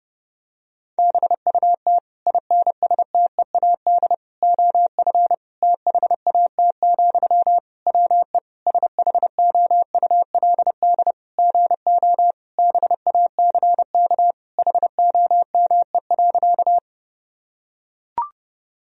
Morse code, heard as a keyed tone, about 30 wpm, 700 Hz. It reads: BUT INSTEAD OF THAT, WE SHOULD GO BACK HOME.
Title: Sentences from Top 1000 Words - Review Part 4 - 30wpm